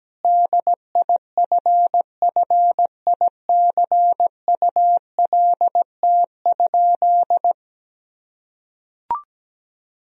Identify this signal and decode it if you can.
Morse code, heard as a keyed tone, about 17 wpm, 700 Hz. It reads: DIFFICULT?
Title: Sentences from Top 1000 Words - Review Part 4 - 17wpm